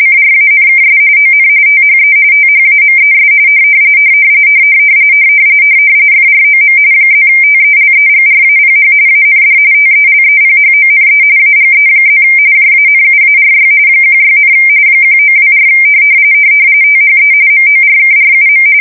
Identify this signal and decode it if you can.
RTTY, about 45.45 baud. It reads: CRANKS ON THE THAMES FROM +TELEGRAPH AND TELEPHONE AGE+ JANUARY 16, 1924 THE MORKRUM
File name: fsk170 / morkrum